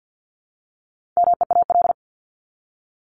Morse code, heard as a keyed tone, about 50 wpm, 700 Hz. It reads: MEAL
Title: Top 1751-2000 Words 50wpm